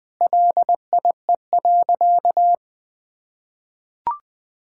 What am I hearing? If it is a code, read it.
Morse code, heard as a keyed tone, about 19 wpm, 700 Hz. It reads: LIE.
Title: Sentences from Top 1600 Words 20wpm